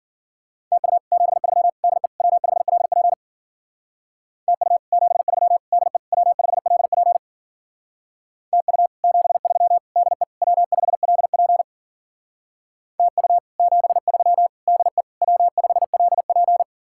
Morse code, heard as a keyed tone, about 50 wpm, 700 Hz. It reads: TU 73 DE W5LP TU 73 DE W5LP TU 73 DE W5LP TU 73 DE W5LP
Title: Speed-Racing Common QSO Phrases Encore - 40wpm